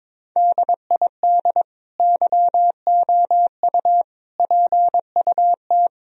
Morse code, heard as a keyed tone, about 22 wpm, 700 Hz. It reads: DID YOU PUT
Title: Sentences from Top 1000 Words - Review Part 4 - 22wpm